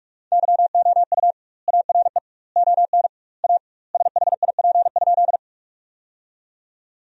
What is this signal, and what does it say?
Morse code, heard as a keyed tone, about 44 wpm, 700 Hz. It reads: YOU ARE ON A SHIP?
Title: 2x Word-Spacing - Sentences from Top 400 Words - 45wpm